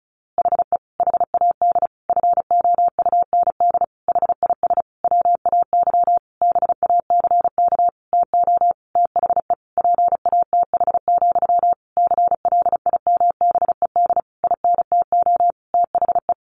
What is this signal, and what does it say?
Morse code, heard as a keyed tone, about 34 wpm, 700 Hz. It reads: HE HAD FOUND HIS WAY BACK TO THE PATH, CLIMBED INTO THE
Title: A Sound Of Thunder 35wpm